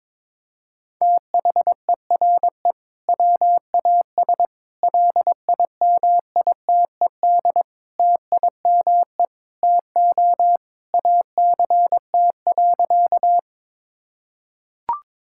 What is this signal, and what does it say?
Morse code, heard as a keyed tone, about 22 wpm, 700 Hz. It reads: THERE WAS LIMITED TIME TO ACT.
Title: Sentences from Top 1000 Words - Review Part 6 - 22wpm